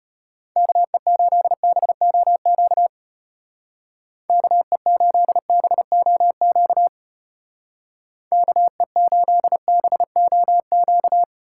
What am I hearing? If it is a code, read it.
Morse code, heard as a keyed tone, about 30 wpm, 700 Hz. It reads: KE8BOQ KE8BOQ KE8BOQ
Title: Speed Racing - USA 2x3 Call Signs 25wpm